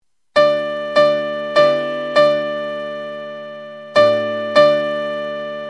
There is no code.